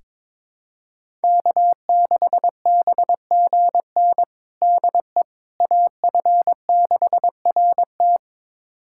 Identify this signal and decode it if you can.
Morse code, heard as a keyed tone, about 22 wpm, 700 Hz. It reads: K6BGN DE AF6RT
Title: USA Callsign DE Callsign - 5 Characters - 22wpm